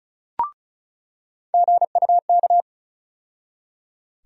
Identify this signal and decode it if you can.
Morse code, heard as a keyed tone, about 34 wpm, 700 Hz. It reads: GUK
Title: USA Call Sign Postfixes - 3 Letters 35wpm